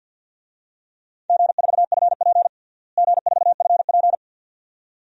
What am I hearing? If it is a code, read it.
Morse code, heard as a keyed tone, about 47 wpm, 700 Hz. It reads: G4FP G4FP
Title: DXCC-100 Call Signs - 4 Characters - Repeated 2x - 50wpm